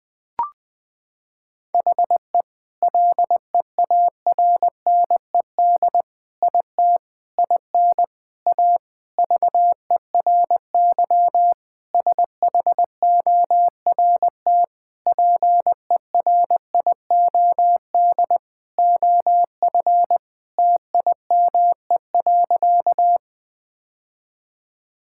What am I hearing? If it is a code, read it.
Morse code, heard as a keyed tone, about 20 wpm, 700 Hz. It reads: HE LEARNED IT IN A VERY SHORT PERIOD OF TIME.
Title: Sentences with 7-Word Phrases - 20wpm